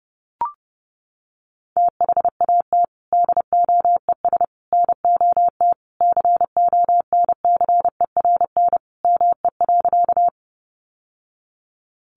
Morse code, heard as a keyed tone, about 30 wpm, 700 Hz. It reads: THAT DOES NOT CONCERN ME.
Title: Sentences from Top 1000 Words 30wpm